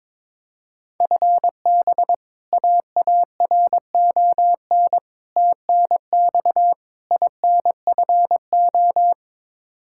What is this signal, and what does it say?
Morse code, heard as a keyed tone, about 22 wpm, 700 Hz. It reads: FB AARON TNX INFO
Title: Practice QSO - Exchange 3 - More Info - 22wpm